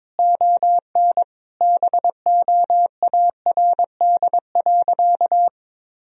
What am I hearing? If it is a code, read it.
Morse code, heard as a keyed tone, about 22 wpm, 700 Hz. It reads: ON BOARD.
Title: Sentences from Top 1000 Words - Review Part 4 - 22wpm